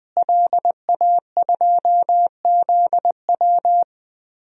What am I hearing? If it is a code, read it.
Morse code, heard as a keyed tone, about 21 wpm, 700 Hz. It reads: LA2ZW
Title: DXCC-100 Call Signs - 5 Characters - Repeated 2x - 20wpm